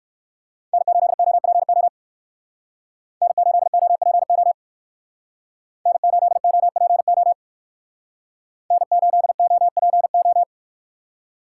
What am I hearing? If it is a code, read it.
Morse code, heard as a keyed tone, about 62 wpm, 700 Hz. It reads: N8OPO N8OPO N8OPO N8OPO